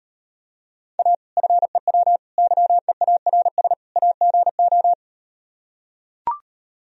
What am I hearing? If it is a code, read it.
Morse code, heard as a keyed tone, about 38 wpm, 700 Hz. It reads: A FEW YEARS AGO